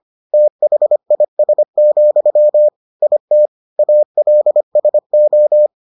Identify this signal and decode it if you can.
Morse code, heard as a keyed tone, about 25 wpm, 600 Hz. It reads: THIS, IT ALSO